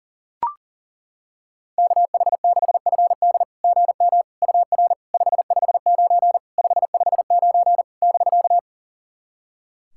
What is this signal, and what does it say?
Morse code, heard as a keyed tone, about 40 wpm, 690 Hz. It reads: KH6FD GM UR 559 559 <BK>
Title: SOTA QSO Practice 40wpm